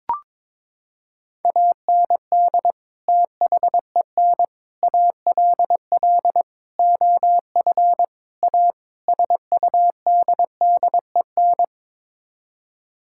Morse code, heard as a keyed tone, about 22 wpm, 700 Hz. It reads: AND THEN ALL OF A SUDDEN
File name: practice / 22wpm / Sets of 6 Words - 22wpm